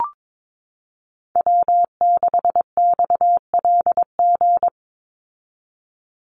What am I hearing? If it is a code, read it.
Morse code, heard as a keyed tone, about 22 wpm, 700 Hz. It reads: W6XLG
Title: USA Call Signs 22wpm